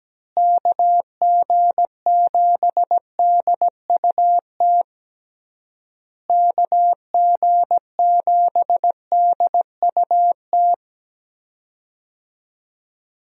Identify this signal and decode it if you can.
Morse code, heard as a keyed tone, about 17 wpm, 700 Hz. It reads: KG7DUT KG7DUT
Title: USA Call Signs - 6 Characters - Repeated 3x - 17wpm